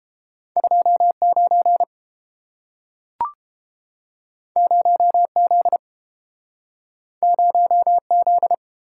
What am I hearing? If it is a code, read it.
Morse code, heard as a keyed tone, about 30 wpm, 700 Hz. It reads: 29 0Z 0Z